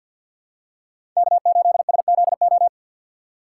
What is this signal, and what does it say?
Morse code, heard as a keyed tone, about 50 wpm, 700 Hz. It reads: K8SZO
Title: USA Call Signs - 5 Characters - 4x Character Spacing - 50wpm